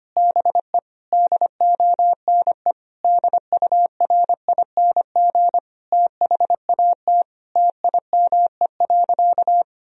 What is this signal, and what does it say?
Morse code, heard as a keyed tone, about 25 wpm, 700 Hz. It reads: BE DONE DURING THAT TIME.
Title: Sentences from Top 1000 Words - Review Part 2 - 25wpm